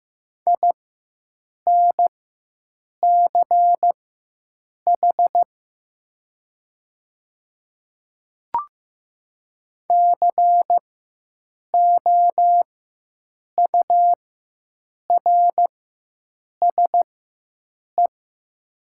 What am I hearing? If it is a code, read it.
Morse code, heard as a keyed tone, about 15 wpm, spaced as for 7 wpm, 700 Hz. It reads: INCH COURSE
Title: 4x Character Spacing - Top 500 Words - 15wpm